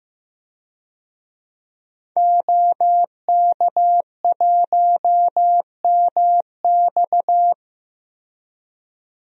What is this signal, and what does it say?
Morse code, heard as a keyed tone, about 16 wpm, 700 Hz. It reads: OK1MX
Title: DXCC-100 Call Signs - 5 Characters - 15wpm